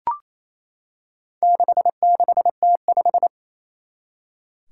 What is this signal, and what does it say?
Morse code, heard as a keyed tone, about 28 wpm, 700 Hz. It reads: 66T5
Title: Four Challenging Letter-Number Combinations 28wpm